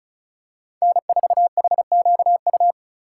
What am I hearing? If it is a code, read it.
Morse code, heard as a keyed tone, about 34 wpm, 700 Hz. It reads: N4HQU
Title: USA Call Signs - 5 Characters - 4x Character Spacing - 35wpm